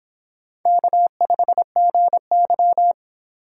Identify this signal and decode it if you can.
Morse code, heard as a keyed tone, about 25 wpm, 700 Hz. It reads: K5GY